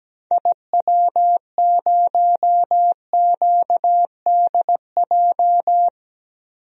Morse code, heard as a keyed tone, about 17 wpm, 700 Hz. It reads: IW0QDJ
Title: DXCC-100 Call Signs - 6 Characters - 17wpm